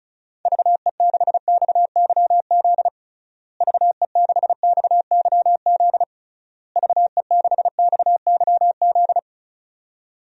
Morse code, heard as a keyed tone, about 36 wpm, 700 Hz. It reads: VE6XYZ VE6XYZ VE6XYZ
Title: DXCC-100 Call Signs - 6 Characters - Repeated 3x - 35wpm